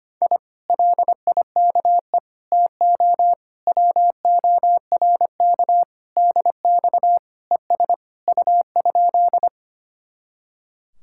Morse code, heard as a keyed tone, about 24 wpm, 700 Hz. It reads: I LIKE TO WORK DX ES U?
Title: Speed-Racing Common QSO Phrases - 25wpm